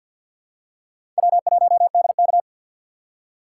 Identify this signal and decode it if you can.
Morse code, heard as a keyed tone, about 53 wpm, 700 Hz. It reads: W1DK
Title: USA Call Signs - 2x Character Spacing - 50wpm